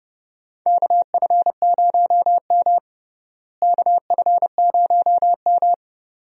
Morse code, heard as a keyed tone, about 30 wpm, 700 Hz. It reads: KF0M KF0M
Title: USA Call Signs - 4 Characters - Repeated 2x - 30wpm